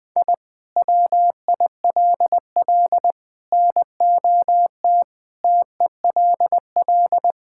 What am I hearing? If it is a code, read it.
Morse code, heard as a keyed tone, about 20 wpm, 700 Hz. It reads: I WILL NOT TELL
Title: Sentences from Top 1000 Words - Review Part 5 - 20wpm